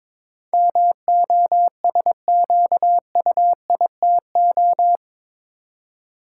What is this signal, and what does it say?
Morse code, heard as a keyed tone, about 22 wpm, 700 Hz. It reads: MOSQUITO
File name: practice / 22wpm / Eight Letter Words - 22wpm